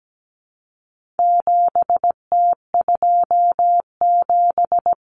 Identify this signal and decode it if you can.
Morse code, heard as a keyed tone, about 17 wpm, 700 Hz. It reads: 7T27
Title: Common Cut Numbers - 4-Digit 17wpm